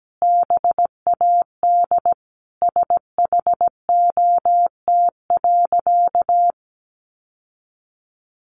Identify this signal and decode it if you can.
Morse code, heard as a keyed tone, about 17 wpm, 700 Hz. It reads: BAD SHOT.
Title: Sentences from Top 1100 Words 17wpm